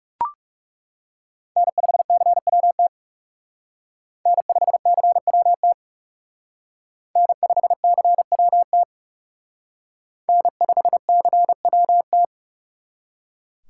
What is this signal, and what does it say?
Morse code, heard as a keyed tone, about 37 wpm, 700 Hz. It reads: N5CWT N5CWT N5CWT N5CWT
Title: Speed Racing - USA 1x3 Call Signs 30wpm